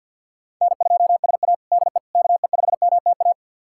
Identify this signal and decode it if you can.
Morse code, heard as a keyed tone, about 51 wpm, 700 Hz. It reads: N2SA DE KE5MTA
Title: USA Callsign DE Callsign 50wpm